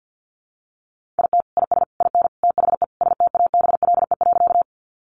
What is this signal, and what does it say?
Morse code, heard as a keyed tone, about 50 wpm, 700 Hz. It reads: IT IS IN THE STABLE.
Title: Sentences from Top 1500 Words 50wpm